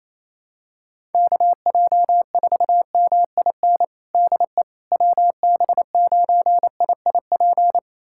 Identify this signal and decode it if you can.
Morse code, heard as a keyed tone, about 27 wpm, 700 Hz. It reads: KJ4MIN DE WB9IIP
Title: USA Callsign DE Callsign - 6 Characters - 28wpm